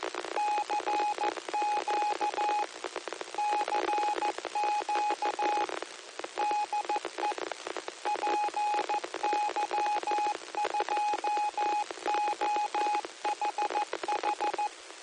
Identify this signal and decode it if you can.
Morse code, heard as a keyed tone, about 14 wpm, 840 Hz, with radio noise. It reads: CQ CQ DE PY2OHH